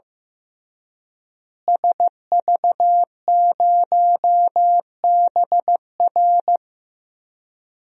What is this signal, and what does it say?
Morse code, heard as a keyed tone, about 15 wpm, 700 Hz. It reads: SV0BR
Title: DXCC-100 Call Signs - 5 Characters - Repeated 3x - 15wpm